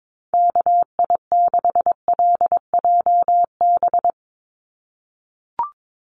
Morse code, heard as a keyed tone, about 22 wpm, 700 Hz. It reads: KI6LJB
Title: USA Call Signs 22wpm